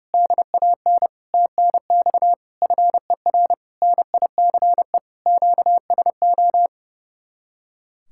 Morse code, heard as a keyed tone, about 32 wpm, 700 Hz. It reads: DAN TNX FER NICE QSO
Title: Speed-Racing Common QSO Phrases Encore - 30wpm